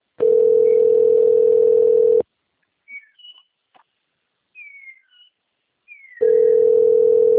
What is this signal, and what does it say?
Ringback tone (North America), 440 + 480 Hz.
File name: rg-1600-unknown- - - .19290